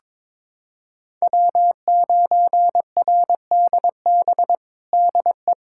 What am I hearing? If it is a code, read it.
Morse code, heard as a keyed tone, about 22 wpm, 700 Hz. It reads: W9RDB DE
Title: USA Callsign DE Callsign - 5 Characters - 22wpm